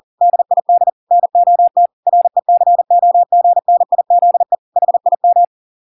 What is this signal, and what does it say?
Morse code, heard as a keyed tone, about 40 wpm, 700 Hz. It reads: DID NOT RECOGNIZE HIM